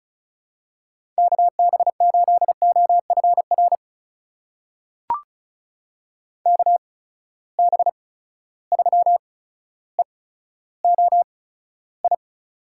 Morse code, heard as a keyed tone, about 36 wpm, spaced as for 15 wpm, 700 Hz. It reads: KB8OFR KB3EOI
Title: USA Call Signs - 6 Characters - 8x Character Spacing - 35wpm